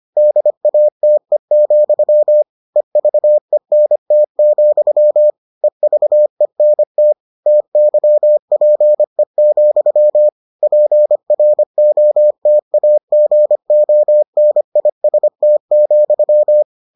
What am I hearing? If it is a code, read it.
Morse code, heard as a keyed tone, about 25 wpm, 600 Hz. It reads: DATE, EVENT, EVENT TYPE, PROTAGONIST,